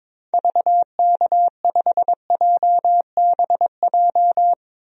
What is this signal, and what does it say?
Morse code, heard as a keyed tone, about 22 wpm, 700 Hz. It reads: VK5JBJ
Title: DXCC-100 Call Signs - 6 Characters - 22wpm